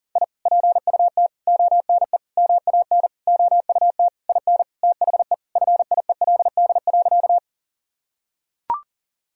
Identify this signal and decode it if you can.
Morse code, heard as a keyed tone, about 40 wpm, 700 Hz. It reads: I PUT ONE MAN OUT IN THE FIELD.
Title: Sentences from Top 1000 Words - Review Part 2 - 40wpm